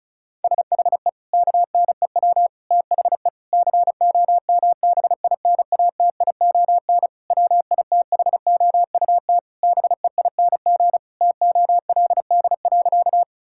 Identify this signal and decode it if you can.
Morse code, heard as a keyed tone, about 36 wpm, 700 Hz. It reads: SHE KNEW THE COMBINATION WITHOUT BEING TOLD.
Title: Sentences from Top 1000 Words - Review Part 6 - 35wpm